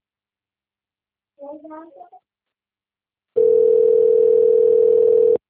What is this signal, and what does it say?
Ringback tone (North America), 440 + 480 Hz, with speech.
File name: rg-1600-unknown- - - .9697